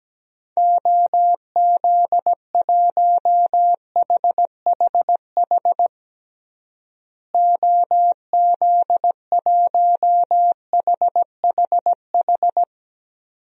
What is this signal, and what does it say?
Morse code, heard as a keyed tone, about 17 wpm, 700 Hz. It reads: OZ1HHH OZ1HHH
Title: DXCC-100 Call Signs - 6 Characters - Repeated 2x - 17wpm